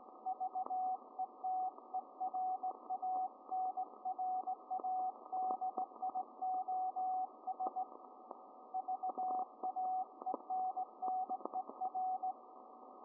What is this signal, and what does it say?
Morse code, heard as a keyed tone, about 18 wpm, 750 Hz, with radio noise. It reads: VETERANRADIOS VAENNER